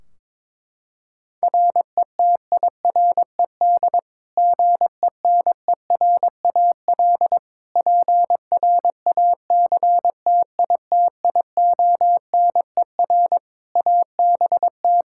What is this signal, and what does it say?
Morse code, heard as a keyed tone, about 22 wpm, 700 Hz. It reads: RETIRED GENERAL PRACTITIONER ABT
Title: Practice QSO - Exchange 3 - More Info - 22wpm